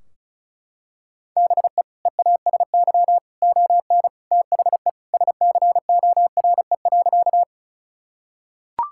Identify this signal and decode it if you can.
Morse code, heard as a keyed tone, about 36 wpm, 700 Hz. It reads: BE EASY ON THE SCORE.
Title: Sentences from Top 1600 Words 35wpm